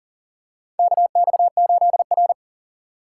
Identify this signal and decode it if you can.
Morse code, heard as a keyed tone, about 40 wpm, 700 Hz. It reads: KX8R
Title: USA Call Signs - 4 Characters - 2x Character Spacing - 40wpm